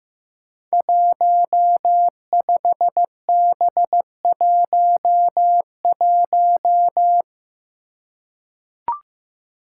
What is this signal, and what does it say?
Morse code, heard as a keyed tone, about 15 wpm, 700 Hz. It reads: 15B11
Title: Five Challenging Letter-Number Combinations 15wpm